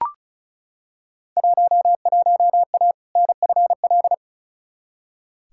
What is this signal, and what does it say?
Morse code, heard as a keyed tone, about 36 wpm, 700 Hz. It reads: 11A NFL
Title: Field Day Exchanges Encore - 35wpm